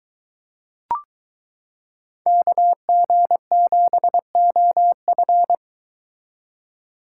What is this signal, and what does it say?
Morse code, heard as a keyed tone, about 22 wpm, 700 Hz. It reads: KG7OF